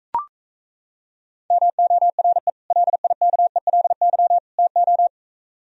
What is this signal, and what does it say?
Morse code, heard as a keyed tone, about 37 wpm, 700 Hz. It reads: MORE LIKELY TO